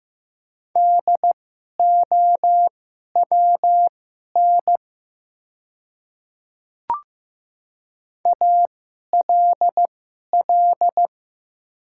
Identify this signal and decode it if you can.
Morse code, heard as a keyed tone, about 15 wpm, spaced as for 11 wpm, 700 Hz. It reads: DOWN ALL
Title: 2x Character Spacing - Top 100 Words - 15wpm